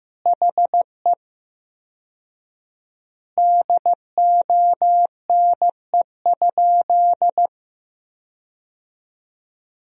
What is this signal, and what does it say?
Morse code, heard as a keyed tone, about 15 wpm, 700 Hz. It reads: HE DONE?
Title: 4x Word-Spacing - Sentences from Top 500 Words - 15wpm